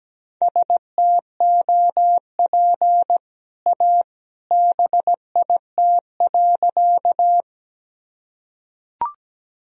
Morse code, heard as a keyed tone, about 17 wpm, 700 Hz. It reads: STOP A BIT.
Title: Sentences from Top 1000 Words - Review Part 1 - 17wpm